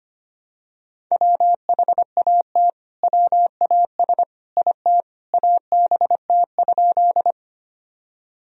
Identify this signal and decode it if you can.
Morse code, heard as a keyed tone, about 25 wpm, 700 Hz. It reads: WHAT WAS IT ABT?
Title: Sentences from Top 100 Words and Abbreviations 25wpm